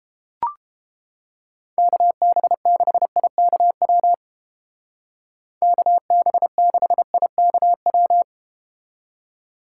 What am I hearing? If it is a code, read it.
Morse code, heard as a keyed tone, about 32 wpm, 700 Hz. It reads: KB6IKW KB6IKW